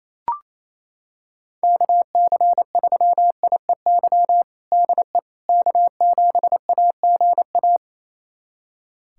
Morse code, heard as a keyed tone, about 28 wpm, 700 Hz. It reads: KC3IEY DE K7AGA
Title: USA Callsign DE Callsign 28wpm